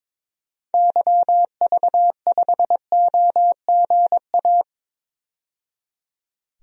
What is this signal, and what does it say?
Morse code, heard as a keyed tone, about 22 wpm, 700 Hz. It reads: YV5OGA
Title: DXCC-100 Call Signs - 6 Characters - 22wpm